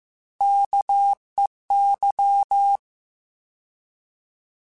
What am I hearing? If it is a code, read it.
Morse code, heard as a keyed tone, about 15 wpm, 800 Hz. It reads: KEY
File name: Alphabetical / Alphabetical mp3s / Ks @13